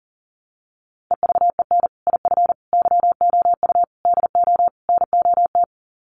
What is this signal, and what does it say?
Morse code, heard as a keyed tone, about 37 wpm, 700 Hz. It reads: EVEN IF YOU DO NOT
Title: Sets of 5 Words Encore 40wpm